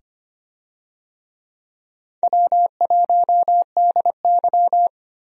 Morse code, heard as a keyed tone, about 25 wpm, 700 Hz. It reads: W1DY
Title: USA Call Signs - 4 Characters - 25wpm